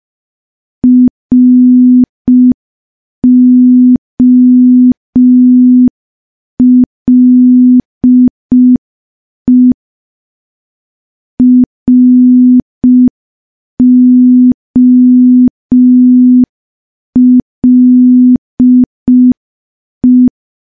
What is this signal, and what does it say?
Morse code, heard as a keyed tone, about 5 wpm, 250 Hz. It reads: ROLE ROLE
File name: morse / morse-training-05WPM